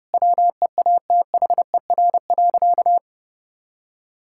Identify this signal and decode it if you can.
Morse code, heard as a keyed tone, about 30 wpm, 700 Hz. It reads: WEATHER.
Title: Sentences from Top 1000 Words - Review Part 5 - 30wpm